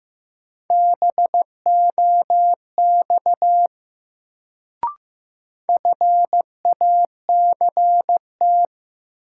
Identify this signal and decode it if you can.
Morse code, heard as a keyed tone, about 15 wpm, 700 Hz. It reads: BOX FACT
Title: Sentences from Top 500 Words - New Word Drill - 15wpm